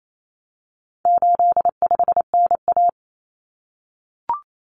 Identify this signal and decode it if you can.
Morse code, heard as a keyed tone, about 28 wpm, 700 Hz. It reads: 85NA
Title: Common Cut Numbers - 4-Digit 28wpm